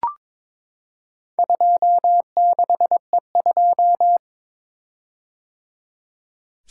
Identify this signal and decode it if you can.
Morse code, heard as a keyed tone, about 22 wpm, 700 Hz. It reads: 26E2
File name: practice / 22wpm / Four Letter-Number Combinations - 22wpm